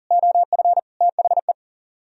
Morse code, heard as a keyed tone, about 51 wpm, 700 Hz. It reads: OF THE